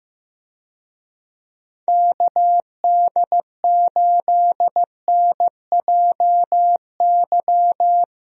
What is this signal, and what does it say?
Morse code, heard as a keyed tone, about 16 wpm, 700 Hz. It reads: KD8NJY
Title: USA Call Signs - 6 Characters - Repeated 2x - 15wpm